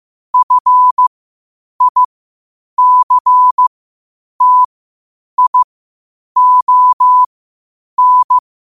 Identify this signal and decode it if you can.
Morse code, heard as a keyed tone, about 15 wpm, spaced as for 8 wpm, 1000 Hz. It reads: FICTION
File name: qotdfiles / 8wpm